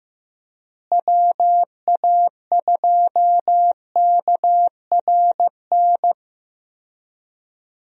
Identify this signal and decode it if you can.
Morse code, heard as a keyed tone, about 15 wpm, 700 Hz. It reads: WA2KRN
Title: USA Call Signs - 6 Characters - Repeated 2x - 15wpm